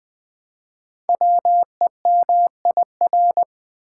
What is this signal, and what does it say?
Morse code, heard as a keyed tone, about 20 wpm, 700 Hz. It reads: WEMIR
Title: Code Groups - 5 Letters - 20wpm